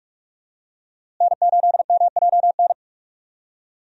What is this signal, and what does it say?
Morse code, heard as a keyed tone, about 41 wpm, 700 Hz. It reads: N8MJN
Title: USA Call Signs - 5 Characters - Repeated 3x - 45wpm